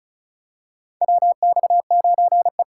Morse code, heard as a keyed tone, about 36 wpm, 700 Hz. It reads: WX9E
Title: USA Call Signs - 4 Characters - 2x Character Spacing - 35wpm